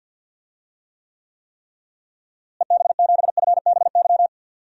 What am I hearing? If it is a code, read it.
Morse code, heard as a keyed tone, about 51 wpm, 700 Hz. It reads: EB7FBY
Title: DXCC-100 Call Signs - 6 Characters - Repeated 3x - 50wpm